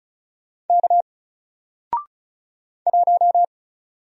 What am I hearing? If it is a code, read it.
Morse code, heard as a keyed tone, about 36 wpm, 700 Hz. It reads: K 1
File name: Single-Letter-Number-35wpm